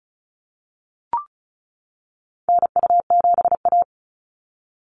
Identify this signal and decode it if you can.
Morse code, heard as a keyed tone, about 36 wpm, 700 Hz. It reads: NU7A
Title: NA SOTA Activator Callsigns 35wpm